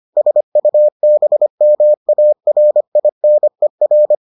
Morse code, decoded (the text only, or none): SUBMARINER